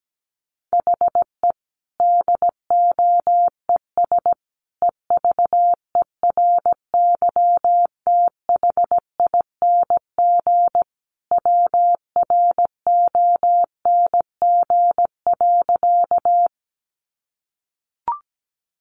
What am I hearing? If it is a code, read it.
Morse code, heard as a keyed tone, about 17 wpm, 700 Hz. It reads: HE DOES EVERYTHING WRONG.
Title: Sentences from Top 900 Words 17wpm